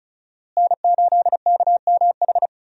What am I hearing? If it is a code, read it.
Morse code, heard as a keyed tone, about 36 wpm, 700 Hz. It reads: N8KMH
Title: USA Call Signs - 5 Characters - Repeated 3x - 35wpm